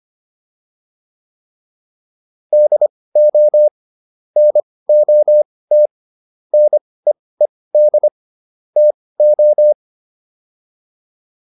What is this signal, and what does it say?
Morse code, heard as a keyed tone, about 25 wpm, spaced as for 18 wpm, 600 Hz. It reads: DO NOT NEED TO